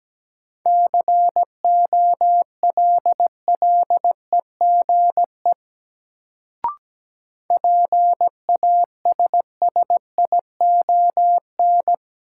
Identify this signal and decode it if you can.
Morse code, heard as a keyed tone, about 17 wpm, 700 Hz. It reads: COLLEGE PASSION
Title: Sentences from Top 1300 Words - New Word Drill Alt - 17wpm